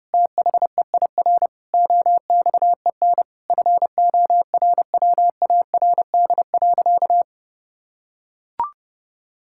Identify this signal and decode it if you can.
Morse code, heard as a keyed tone, about 30 wpm, 700 Hz. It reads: THEIR OXEN FORWARD.
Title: Pangram Challenge v2 - 30wpm